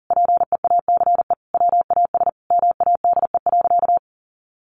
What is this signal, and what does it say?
Morse code, heard as a keyed tone, about 40 wpm, 700 Hz. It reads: PEACE WAS MADE.
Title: Sentences from Top 1500 Words 40wpm